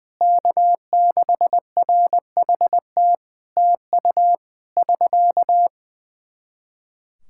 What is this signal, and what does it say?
Morse code, heard as a keyed tone, about 20 wpm, 700 Hz. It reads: K6RHT TU <SK>
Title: Practice QSO - Exchange 4 - Ending - 20wpm